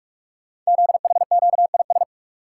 Morse code, heard as a keyed tone, about 44 wpm, 700 Hz. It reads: ZHQIS
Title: Code Groups - 5 Letters - 45wpm